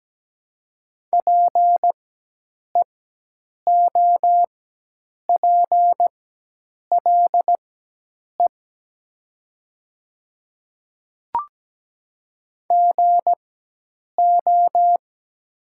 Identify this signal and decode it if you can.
Morse code, heard as a keyed tone, about 17 wpm, spaced as for 8 wpm, 700 Hz. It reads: PEOPLE GO
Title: 4x Character Spacing - Top 500 Words - 17wpm